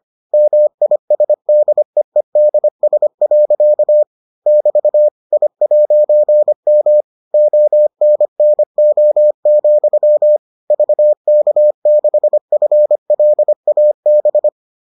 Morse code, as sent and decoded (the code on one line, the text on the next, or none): -- .. ... -.. . -.. ... .-.-.- / -...- / .. .----. -- / --- -. -. --- --..-- / ...- -.- -.... ..-. .-.. .- -...
MISDEEDS. <BT> I'M ONNO, VK6FLAB